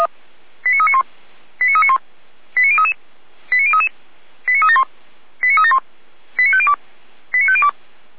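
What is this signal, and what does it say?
Five-tone selective call (ZVEI), heard as quick runs of five tones, tones of about 80 ms. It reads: 89391 89391 89033 89033 89371 89371 89502 89502
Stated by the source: Five-tone ZVEI selective call received with an SDR.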